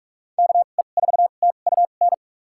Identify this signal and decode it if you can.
Morse code, heard as a keyed tone, about 46 wpm, spaced as for 32 wpm, 700 Hz. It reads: KE4TUN
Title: USA Call Signs - 6 Characters - 2x Character Spacing - 45wpm